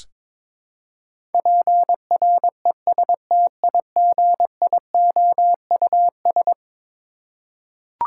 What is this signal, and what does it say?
Morse code, heard as a keyed tone, about 22 wpm, 700 Hz. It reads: PRESTIGIOUS
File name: practice / 22wpm / Eleven Letter Words - 22wpm